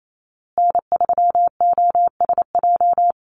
Morse code, heard as a keyed tone, about 28 wpm, 700 Hz. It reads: N3OSJ
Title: USA Call Signs Repeated 2x 28wpm